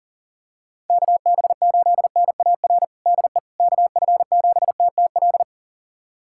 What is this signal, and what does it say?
Morse code, heard as a keyed tone, about 40 wpm, 700 Hz. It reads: KB8NAR DE KF7TTL
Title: USA Callsign DE Callsign - 6 Characters - 40wpm